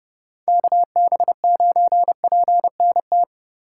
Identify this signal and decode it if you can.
Morse code, heard as a keyed tone, about 30 wpm, 700 Hz. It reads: KB9PNT
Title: USA Call Signs - 8x Character Spacing - 30wpm